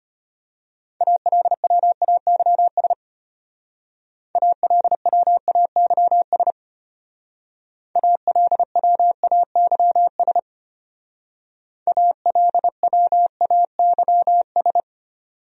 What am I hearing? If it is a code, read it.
Morse code, heard as a keyed tone, about 30 wpm, 700 Hz. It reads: ALWAYS ALWAYS ALWAYS ALWAYS
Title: Speed-Racing Top 500 Words 25wpm